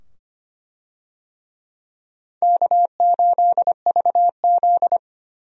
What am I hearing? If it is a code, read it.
Morse code, heard as a keyed tone, about 25 wpm, 700 Hz. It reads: K8VZ
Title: USA Call Signs - 4 Characters - 25wpm